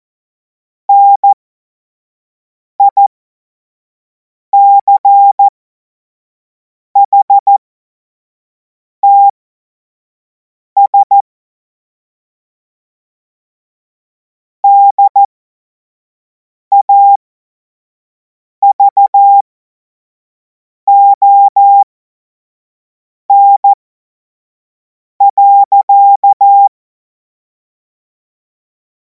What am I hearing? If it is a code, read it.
Morse code, heard as a keyed tone, about 14 wpm, spaced as for 5 wpm, 800 Hz. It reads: NICHTS DAVON.
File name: Morsekurs / Witze / Witz 17. März 2026 25 F70